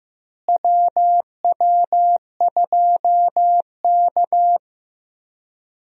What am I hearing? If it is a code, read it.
Morse code, heard as a keyed tone, about 15 wpm, 700 Hz. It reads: WW2K
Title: USA Call Signs - 4 Characters - Repeated 3x - 15wpm